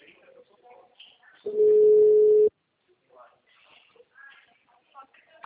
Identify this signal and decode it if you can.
Ringback tone (Europe, 425 Hz), with speech.